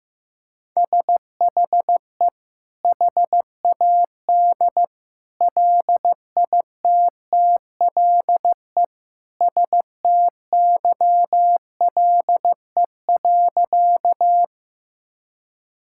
Morse code, heard as a keyed tone, about 15 wpm, 700 Hz. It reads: SHE HAD LITTLE STYLE.
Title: Sentences from Top 1000 Words - Review Part 5 - 15wpm